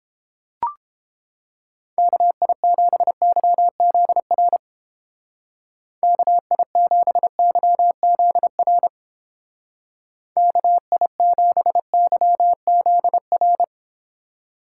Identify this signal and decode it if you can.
Morse code, heard as a keyed tone, about 29 wpm, 700 Hz. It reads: KI7YZR KI7YZR KI7YZR